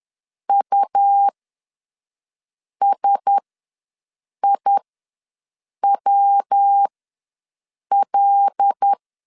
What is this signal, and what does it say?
Morse code, heard as a keyed tone, about 10 wpm, spaced as for 5 wpm, 800 Hz. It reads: USIWL
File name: LEZIONI / 6 PKX0 / 6 P ws 10wpm 3751